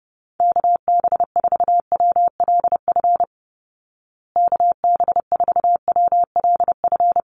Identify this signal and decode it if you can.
Morse code, heard as a keyed tone, about 30 wpm, 700 Hz. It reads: KB4WLF KB4WLF
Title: Call Signs Repeated 3x - 30wpm